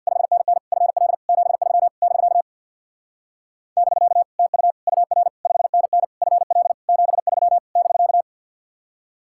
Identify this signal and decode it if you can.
Morse code, heard as a keyed tone, about 62 wpm, 700 Hz. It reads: 5NN FL 73 <BK> <BK> TU UR 5NN FL 73 <BK>
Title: Speed-Racing Common QSO Phrases - 50wpm